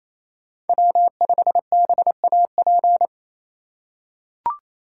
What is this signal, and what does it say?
Morse code, heard as a keyed tone, about 28 wpm, 700 Hz. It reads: W5BAP
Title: USA Call Signs - 5 Characters - 4x Character Spacing - 28wpm